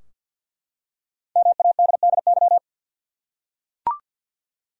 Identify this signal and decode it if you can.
Morse code, heard as a keyed tone, about 47 wpm, 700 Hz. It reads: MADDY
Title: USA Common Names - 50wpm